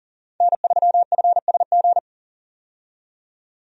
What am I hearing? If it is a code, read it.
Morse code, heard as a keyed tone, about 38 wpm, 700 Hz. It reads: N3FSG
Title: USA Call Signs - 5 Characters - 40wpm